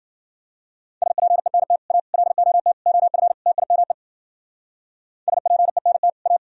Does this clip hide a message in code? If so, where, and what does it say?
Yes: Morse code from 1.0 to 6.4 s, heard as a keyed tone, about 52 wpm, 700 Hz. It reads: SPENT A LOT OF TIME SPENT A